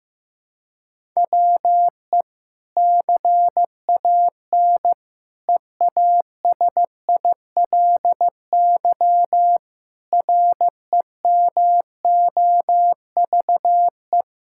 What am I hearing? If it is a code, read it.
Morse code, heard as a keyed tone, about 15 wpm, 700 Hz. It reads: WE CAN EASILY REMOVE
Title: Sentences from Top 1000 Words - Review Part 5 - 15wpm